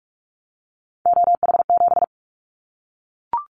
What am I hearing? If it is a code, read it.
Morse code, heard as a keyed tone, about 44 wpm, 700 Hz. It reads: OH7
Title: Callsign Prefixes for Top 25 DXCC - Repeated 3x - 45wpm